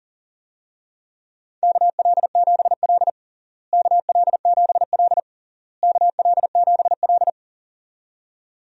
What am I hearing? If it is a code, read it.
Morse code, heard as a keyed tone, about 37 wpm, 700 Hz. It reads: KL7L KL7L KL7L
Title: DXCC-100 Call Signs - 4 Characters - Repeated 3x - 40wpm